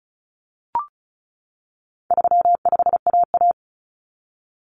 Morse code, heard as a keyed tone, about 34 wpm, 700 Hz. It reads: 35AA
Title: Common Cut Numbers - 4-Digit 35wpm